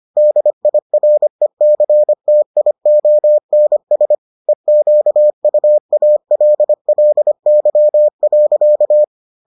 Morse code, heard as a keyed tone, about 25 wpm, 600 Hz. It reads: DIRECTIONS EQUALLY.